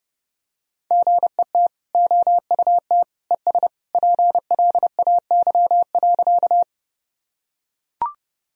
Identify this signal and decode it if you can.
Morse code, heard as a keyed tone, about 32 wpm, 700 Hz. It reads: GET OUT ES PLAY.
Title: Sentences from Top 200 Words and Abbreviations 30wpm